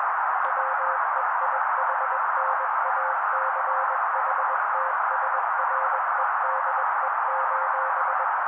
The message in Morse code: WEIHNACHTSREDE: